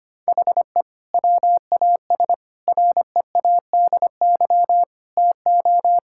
Morse code, heard as a keyed tone, about 25 wpm, 700 Hz. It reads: HE WAS READY TO
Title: Sentences from Top 1000 Words - Review Part 5 - 25wpm